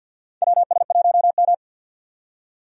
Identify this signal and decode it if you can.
Morse code, heard as a keyed tone, about 47 wpm, 700 Hz. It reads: WS1M
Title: USA Call Signs - 4 Characters - 50wpm